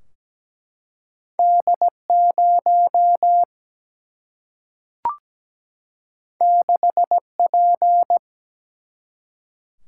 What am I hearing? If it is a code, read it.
Morse code, heard as a keyed tone, about 17 wpm, 700 Hz. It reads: D 0 6 P
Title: Single-Letter Single-Number 17wpm